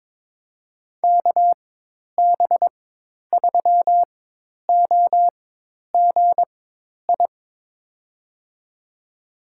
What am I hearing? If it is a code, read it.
Morse code, heard as a keyed tone, about 22 wpm, spaced as for 10 wpm, 700 Hz. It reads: KB3OGI
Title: USA Call Signs - 6 Characters - 4x Character Spacing - 22wpm